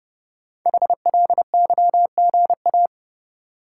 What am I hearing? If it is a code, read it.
Morse code, heard as a keyed tone, about 30 wpm, 700 Hz. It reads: HLYGA
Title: Code Groups - 5 Letters - 30wpm